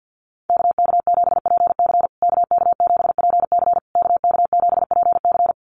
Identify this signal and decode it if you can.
Morse code, heard as a keyed tone, about 50 wpm, 700 Hz. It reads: KK7PC KK7PC KK7PC
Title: USA Call Signs Repeated 3x 50wpm